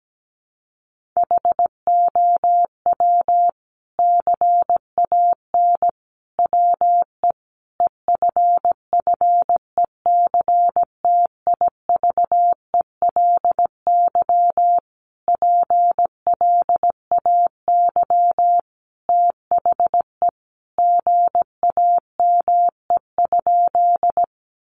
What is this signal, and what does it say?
Morse code, heard as a keyed tone, about 17 wpm, 700 Hz. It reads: HOW CAN WE EFFECTIVELY PLAY THE GAME?
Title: Sentences from Top 1400 Words 17wpm